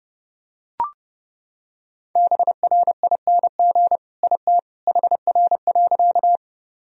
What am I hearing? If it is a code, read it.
Morse code, heard as a keyed tone, about 30 wpm, 700 Hz. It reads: BRING IT HR.
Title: Sentences from Top 500 Words and Abbreviations 30wpm